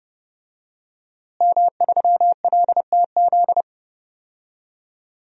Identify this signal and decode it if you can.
Morse code, heard as a keyed tone, about 30 wpm, 700 Hz. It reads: M3LTZ
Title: DXCC-100 Call Signs - 5 Characters - Repeated 3x - 30wpm